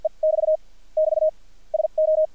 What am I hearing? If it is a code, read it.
Morse code, heard as a keyed tone, about 56 wpm, spaced as for 43 wpm, 620 Hz. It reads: E <BT> <BT> SO
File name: archived / CRSnet